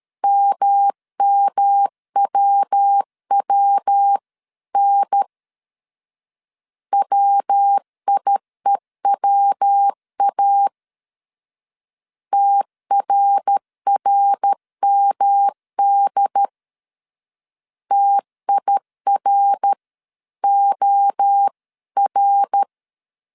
Morse code, heard as a keyed tone, about 13 wpm, 800 Hz. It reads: MMWWN WIEWA TRRMD TIROR
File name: CW / DET / 1 3 1500C S B 12wpm 2828